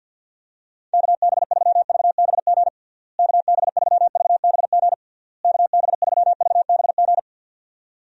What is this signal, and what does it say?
Morse code, heard as a keyed tone, about 53 wpm, 700 Hz. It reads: KB3VBG KB3VBG KB3VBG
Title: USA Call Signs - 6 Characters - Repeated 3x - 50wpm